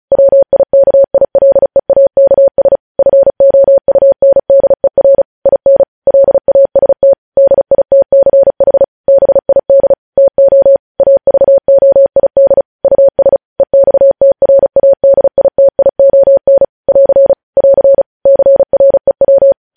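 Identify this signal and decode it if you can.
Morse code, heard as a keyed tone, about 36 wpm, 550 Hz. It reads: WIKILEAKS FOUNDER IN LAST DITCH BID TO AVOID US EXTRADITION <AR> <AR> CREW
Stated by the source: Morse news headlines sent at 35 words per minute.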